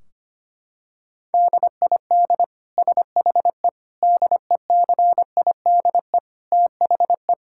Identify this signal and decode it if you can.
Morse code, heard as a keyed tone, about 25 wpm, 700 Hz. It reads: DID SHE DECIDE THE
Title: Sentences from Top 1600 Words 25wpm